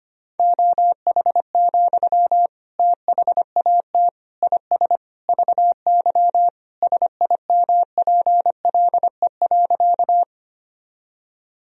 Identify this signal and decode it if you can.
Morse code, heard as a keyed tone, about 25 wpm, 700 Hz. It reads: OH, THAT IS VY SIMPLE.
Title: Sentences from Top 500 Words and Abbreviations 25wpm